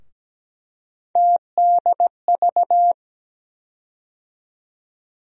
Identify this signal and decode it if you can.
Morse code, heard as a keyed tone, about 17 wpm, 700 Hz. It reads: TDV
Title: USA Call Sign Postfixes - 3 Letters 17wpm